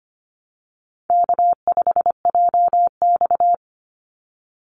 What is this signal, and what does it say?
Morse code, heard as a keyed tone, about 25 wpm, 700 Hz. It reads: K5JX
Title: Call Signs - 25wpm